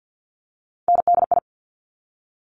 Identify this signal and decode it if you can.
Morse code, heard as a keyed tone, about 50 wpm, 690 Hz. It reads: NDI
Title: Top Three Letter Combinations in Words 50wpm